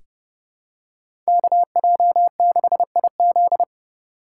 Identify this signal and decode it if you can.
Morse code, heard as a keyed tone, about 30 wpm, 700 Hz. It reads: KJ6IZ
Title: USA Call Signs - 5 Characters - 4x Character Spacing - 30wpm